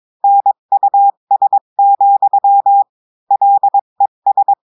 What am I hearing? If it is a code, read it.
Morse code, heard as a keyed tone, about 22 wpm, 800 Hz. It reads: NUS, LES
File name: cours-du-mercredi / audios / cours-853-du-2-mai-2024-etxte chiffre et texte clair-a-20-wpm